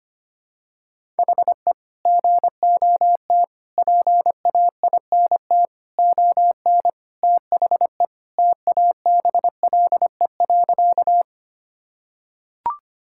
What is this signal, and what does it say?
Morse code, heard as a keyed tone, about 25 wpm, 700 Hz. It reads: HE GOT PAINT ON THE TABLE.
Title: Sentences from Top 1000 Words - Review Part 6 - 25wpm